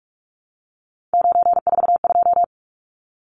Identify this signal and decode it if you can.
Morse code, heard as a keyed tone, about 44 wpm, 700 Hz. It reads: 942
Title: Numbers - 3-Digit 45wpm